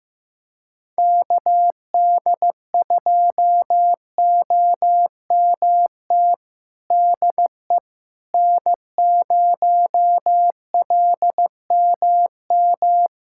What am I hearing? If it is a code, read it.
Morse code, heard as a keyed tone, about 15 wpm, 700 Hz. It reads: KD2OMT DE N0LMM
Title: USA Callsign DE Callsign 15wpm